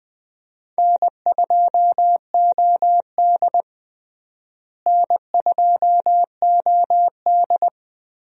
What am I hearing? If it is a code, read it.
Morse code, heard as a keyed tone, about 20 wpm, 700 Hz. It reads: N2OD N2OD
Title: USA Call Signs - 4 Characters - Repeated 3x - 20wpm